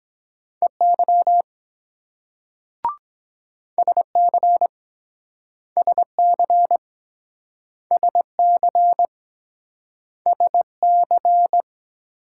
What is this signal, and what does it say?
Morse code, heard as a keyed tone, about 24 wpm, 700 Hz. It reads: EY SC SC SC SC